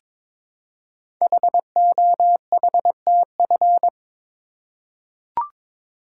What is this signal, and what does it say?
Morse code, heard as a keyed tone, about 22 wpm, 700 Hz. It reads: HOHTF
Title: Code Groups - 5 Letters - 22wpm